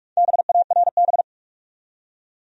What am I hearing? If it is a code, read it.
Morse code, heard as a keyed tone, about 42 wpm, 700 Hz. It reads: BARB
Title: USA Common Names - 45wpm